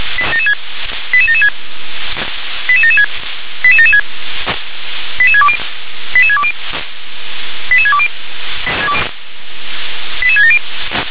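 Five-tone selective call (ZVEI), heard as quick runs of five tones, tones of about 70 ms. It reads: 88776 88776 88776 88620 88620 88620 88670
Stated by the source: ZVEI five-tone call 88774 received via SDR.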